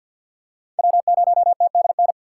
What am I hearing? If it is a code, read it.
Morse code, heard as a keyed tone, about 53 wpm, 700 Hz. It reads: W0TDN
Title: USA Call Signs - 5 Characters - 4x Character Spacing - 50wpm